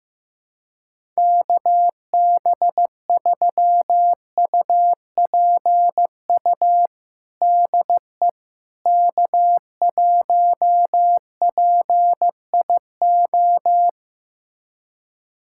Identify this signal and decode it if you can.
Morse code, heard as a keyed tone, about 15 wpm, 700 Hz. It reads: KB3UPU DE K1PIO
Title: USA Callsign DE Callsign 15wpm